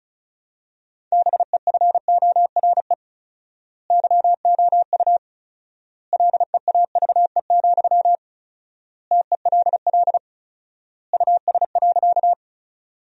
Morse code, heard as a keyed tone, about 34 wpm, 700 Hz. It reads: BEFORE YOU LEAVE, TELL US.
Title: 4x Word-Spacing - Sentences from Top 400 Words - 35wpm